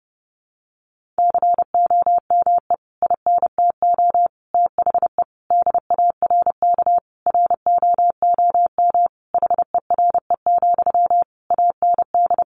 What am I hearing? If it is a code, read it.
Morse code, heard as a keyed tone, about 30 wpm, 700 Hz. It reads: COME INTO THE DARK ROOM HERE, AND
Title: Sentences from Top 1000 Words 30wpm